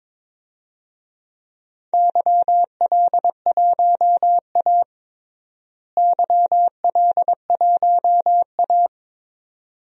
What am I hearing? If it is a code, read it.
Morse code, heard as a keyed tone, about 22 wpm, 700 Hz. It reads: YL1A YL1A
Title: DXCC-100 Call Signs - 4 Characters - Repeated 2x - 22wpm